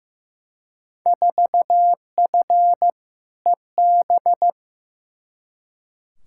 Morse code, heard as a keyed tone, about 15 wpm, 700 Hz. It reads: 4F EB
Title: Field Day Exchanges - 15wpm